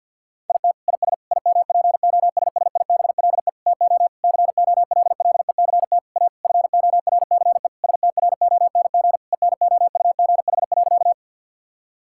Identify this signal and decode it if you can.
Morse code, heard as a keyed tone, about 50 wpm, 700 Hz. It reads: IT IS IMPOSSIBLE TO COLLECT A FORCE STRONG ENOUGH.
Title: Sentences from Top 1600 Words 50wpm